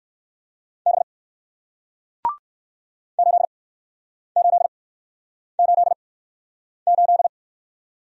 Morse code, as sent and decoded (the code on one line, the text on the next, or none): -... / ---.. ---.. ---.. ---..
B 8888